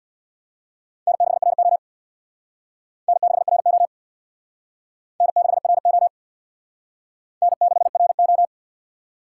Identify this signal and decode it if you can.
Morse code, heard as a keyed tone, about 61 wpm, 690 Hz. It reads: N6RO N6RO N6RO N6RO
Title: Speed Racing - USA 1x2 Call Signs 50wpm